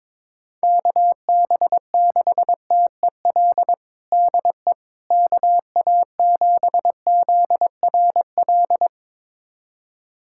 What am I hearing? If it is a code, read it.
Morse code, heard as a keyed tone, about 22 wpm, 700 Hz. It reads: KB6TEL DE KA7ZRL
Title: USA Callsign DE Callsign 22wpm